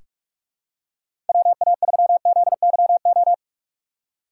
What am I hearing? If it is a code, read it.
Morse code, heard as a keyed tone, about 46 wpm, 700 Hz. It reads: WA3ZYO
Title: USA Call Signs - 6 Characters - 6x Character Spacing - 45wpm